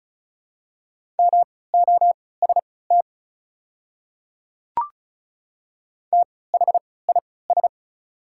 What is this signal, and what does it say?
Morse code, heard as a keyed tone, about 36 wpm, spaced as for 20 wpm, 700 Hz. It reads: MOST THIS
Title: 3x Character Spacing - Top 100 Words - 35wpm